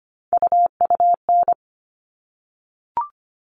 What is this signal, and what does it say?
Morse code, heard as a keyed tone, about 25 wpm, 700 Hz. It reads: UUN
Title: All Cut Numbers - 3-Digit 25wpm